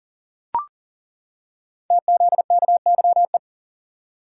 Morse code, heard as a keyed tone, about 40 wpm, 700 Hz. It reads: TZKYE
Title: Code Groups - 5 Letters - 40wpm